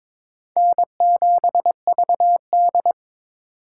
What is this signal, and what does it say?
Morse code, heard as a keyed tone, about 22 wpm, 700 Hz. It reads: N7VD
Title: USA Call Signs - 4 Characters - 4x Character Spacing - 22wpm